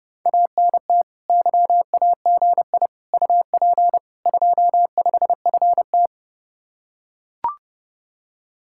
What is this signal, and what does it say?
Morse code, heard as a keyed tone, about 32 wpm, 700 Hz. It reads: ANT YAGI UP 25FT